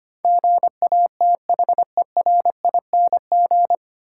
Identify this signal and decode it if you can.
Morse code, heard as a keyed tone, about 25 wpm, 700 Hz. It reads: GATHERING
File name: Nine Letter Words - 25wpm